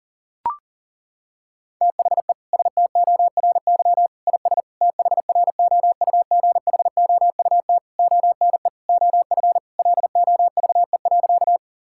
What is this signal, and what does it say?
Morse code, heard as a keyed tone, about 40 wpm, 690 Hz. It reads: THE STORY IS THROUGHOUT ONE OF LOVE.
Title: Sentences from Top 1000 Words - Review Part 5 - 40wpm